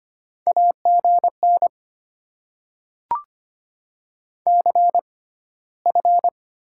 Morse code, heard as a keyed tone, about 25 wpm, 700 Hz. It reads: A G N C F
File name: Top 90 QSO Elements - 6x Character Spacing - 25wpm